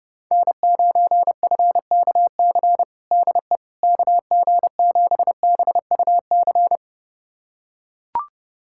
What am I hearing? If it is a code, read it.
Morse code, heard as a keyed tone, about 30 wpm, 700 Hz. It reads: N9FKC DE KG7BUC
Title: USA Callsign DE Callsign 30wpm